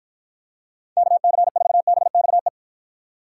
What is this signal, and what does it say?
Morse code, heard as a keyed tone, about 54 wpm, 700 Hz. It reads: KC4BXE